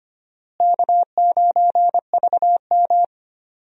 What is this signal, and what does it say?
Morse code, heard as a keyed tone, about 25 wpm, 700 Hz. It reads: K9VM
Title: USA Call Signs - 4 Characters - Repeated 3x - 25wpm